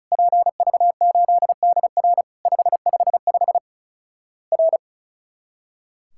Morse code, heard as a keyed tone, about 34 wpm, 700 Hz. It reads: PV8DR 555 R
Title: Worked All Europe DX Contest - QTC Exchange - 35wpm